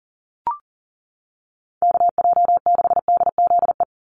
Morse code, heard as a keyed tone, about 40 wpm, 700 Hz. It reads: KJ6DZE
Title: USA Call Signs 40wpm